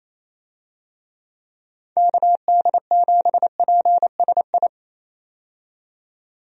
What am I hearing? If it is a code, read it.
Morse code, heard as a keyed tone, about 28 wpm, 700 Hz. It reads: KD7PSI
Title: USA Call Signs - 6 Characters - 28wpm